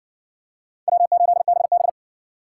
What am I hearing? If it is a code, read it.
Morse code, heard as a keyed tone, about 56 wpm, 700 Hz. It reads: W8BB